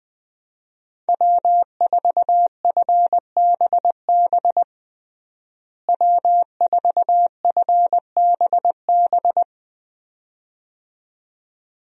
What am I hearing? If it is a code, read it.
Morse code, heard as a keyed tone, about 20 wpm, 700 Hz. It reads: W4FBB W4FBB
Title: USA Call Signs - 5 Characters - Repeated 2x - 20wpm